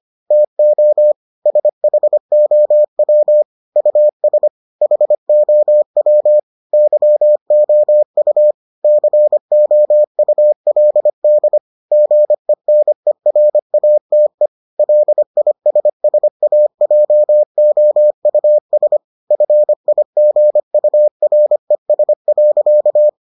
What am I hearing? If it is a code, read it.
Morse code, heard as a keyed tone, about 25 wpm, 600 Hz. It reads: TO SHOW US HOW YOU COULD GENERATE LISSAJOUS FIGURES.